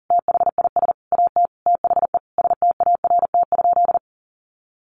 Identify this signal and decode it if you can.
Morse code, heard as a keyed tone, about 37 wpm, 700 Hz. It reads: THIS AT THE START?
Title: Sentences from Top 1500 Words 40wpm